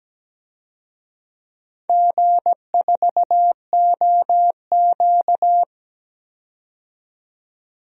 Morse code, heard as a keyed tone, about 17 wpm, 700 Hz. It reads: G4OQ
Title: DXCC-100 Call Signs - 4 Characters - 17wpm